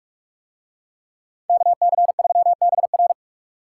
Morse code, heard as a keyed tone, about 44 wpm, 700 Hz. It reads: KC3BR